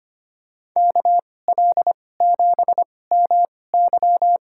Morse code, heard as a keyed tone, about 25 wpm, spaced as for 18 wpm, 700 Hz. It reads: KL7MY
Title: USA Call Signs - 5 Characters - 2x Character Spacing - 25wpm